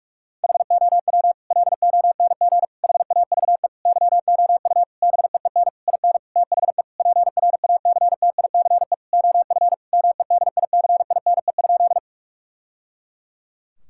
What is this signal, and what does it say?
Morse code, heard as a keyed tone, about 46 wpm, 700 Hz. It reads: HOW LONG HAVE YOU BEEN IN THE PRACTICE OF MEDICINE?
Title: Sentences from Top 1000 Words - Review Part 6 - 45wpm